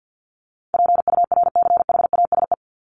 Morse code, heard as a keyed tone, about 50 wpm, 700 Hz. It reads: PURCHASE
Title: Top 501-750 Words 50wpm